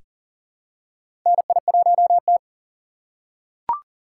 Morse code, heard as a keyed tone, about 39 wpm, 690 Hz. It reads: NI1T
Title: USA Call Signs - 2x Character Spacing - 40wpm